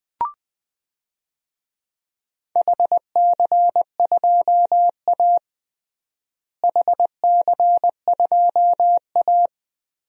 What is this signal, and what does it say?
Morse code, heard as a keyed tone, about 20 wpm, 700 Hz. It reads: HC2A HC2A
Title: DXCC-100 Call Signs - 4 Characters - Repeated 3x - 20wpm